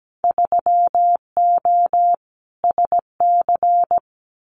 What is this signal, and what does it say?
Morse code, heard as a keyed tone, about 17 wpm, 700 Hz. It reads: 3O SC
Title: Winter Field Day Exchanges Encore 17wpm